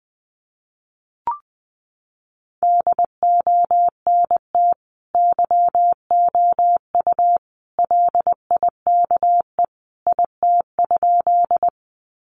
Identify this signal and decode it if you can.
Morse code, heard as a keyed tone, about 20 wpm, 700 Hz. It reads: DONT YOU LIKE IT?
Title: Sentences from Top 300 Words 20wpm